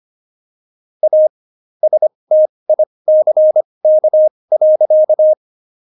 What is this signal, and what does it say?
Morse code, heard as a keyed tone, about 25 wpm, spaced as for 20 wpm, 620 Hz. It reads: A STICK.
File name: Jokes-25-char-20-eff